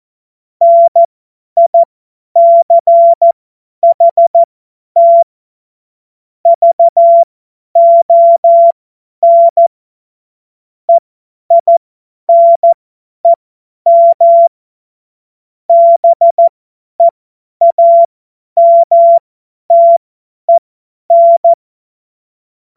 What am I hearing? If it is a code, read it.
Morse code, heard as a keyed tone, about 14 wpm, spaced as for 10 wpm, 680 Hz. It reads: NICHT VON EINEM BEAMTEN
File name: Witz-05-August-2025-50-F70